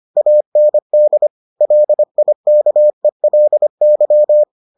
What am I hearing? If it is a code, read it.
Morse code, heard as a keyed tone, about 25 wpm, 600 Hz. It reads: AND LIKELY